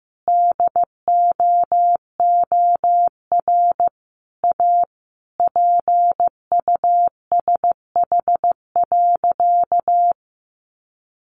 Morse code, heard as a keyed tone, about 15 wpm, 700 Hz. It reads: DOOR A PUSH.
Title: Sentences from Top 1300 Words 15wpm